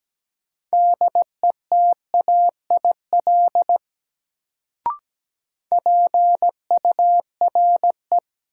Morse code, heard as a keyed tone, about 17 wpm, 700 Hz. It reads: DETAIL PURE
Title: Sentences from Top 1500 Words - New Word Drill Alt - 17wpm